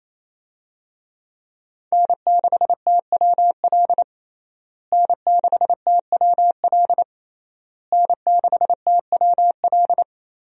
Morse code, heard as a keyed tone, about 28 wpm, 700 Hz. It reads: N6TWL N6TWL N6TWL
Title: USA Call Signs - 5 Characters - Repeated 3x - 28wpm